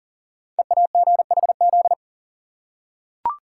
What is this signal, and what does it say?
Morse code, heard as a keyed tone, about 42 wpm, 690 Hz. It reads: EAGHZ
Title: Code Groups - 5 Letters - 40wpm